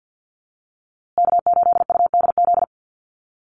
Morse code, heard as a keyed tone, about 50 wpm, 700 Hz. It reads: K8UDZ
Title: USA Call Signs Repeated 3x 50wpm